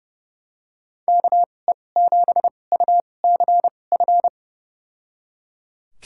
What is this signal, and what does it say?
Morse code, heard as a keyed tone, about 30 wpm, spaced as for 22 wpm, 700 Hz. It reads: KE7UCF
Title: USA Call Signs - 2x Character Spacing - 30wpm